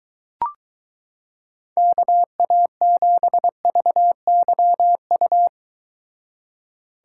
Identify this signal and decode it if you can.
Morse code, heard as a keyed tone, about 22 wpm, 700 Hz. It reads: KA7VYU